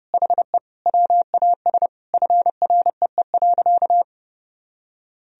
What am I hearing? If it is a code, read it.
Morse code, heard as a keyed tone, about 30 wpm, 700 Hz. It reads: HE WAS FREE.
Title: Sentences from Top 1000 Words - Review Part 4 - 30wpm